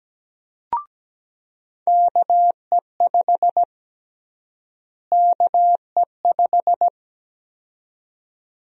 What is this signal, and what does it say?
Morse code, heard as a keyed tone, about 17 wpm, 700 Hz. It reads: KE5 KE5
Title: USA Call Sign Prefixes - Repeated 2x - 17wpm